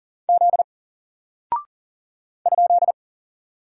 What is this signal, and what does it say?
Morse code, heard as a keyed tone, about 38 wpm, 700 Hz. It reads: Z ?
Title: Single Character (Letter:Number:Punctuation:Prosign) 40wpm